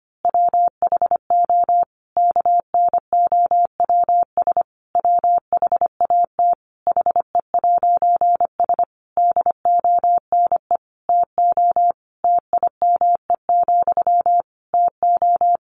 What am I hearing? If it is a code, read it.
Morse code, heard as a keyed tone, about 25 wpm, 700 Hz. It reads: WHO KNOWS WHAT HE'S DONE TO TIME, TO
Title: A Sound Of Thunder 25wpm